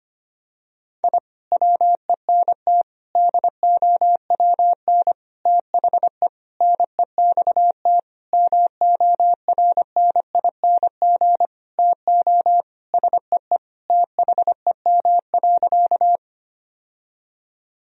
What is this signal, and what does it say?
Morse code, heard as a keyed tone, about 25 wpm, 700 Hz. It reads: I WENT DOWN THE NEXT MORNING TO SEE THEM.
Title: Sentences from Top 1000 Words - Review Part 1 - 25wpm